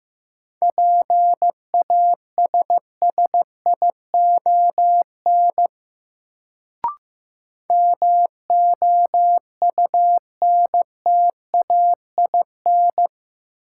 Morse code, heard as a keyed tone, about 15 wpm, 700 Hz. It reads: PASSION MOUNTAIN
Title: Sentences from Top 1300 Words - New Word Drill Alt - 15wpm